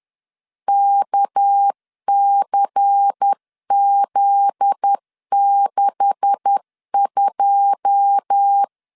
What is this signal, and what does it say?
Morse code, heard as a keyed tone, about 10 wpm, 800 Hz. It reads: KCZ62
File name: LEZIONI / 9 286 10wpm 2735